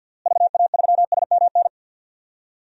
Morse code, heard as a keyed tone, about 50 wpm, 700 Hz. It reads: VA3SMN
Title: DXCC-100 Call Signs - 6 Characters - Repeated 2x - 50wpm